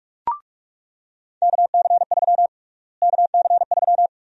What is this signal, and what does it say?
Morse code, heard as a keyed tone, about 42 wpm, 700 Hz. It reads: KC3 KC3
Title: USA Call Sign Prefixes - Repeated 2x - 45wpm